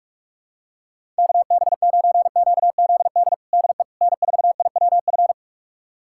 Morse code, heard as a keyed tone, about 42 wpm, 700 Hz. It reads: KB9QZD DE N4IWF
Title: USA Callsign DE Callsign 45wpm